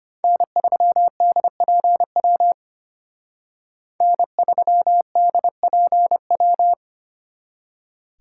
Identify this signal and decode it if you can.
Morse code, heard as a keyed tone, about 31 wpm, 700 Hz. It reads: N3DPW N3DPW
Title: Speed Racing - USA 1x3 Call Signs 25wpm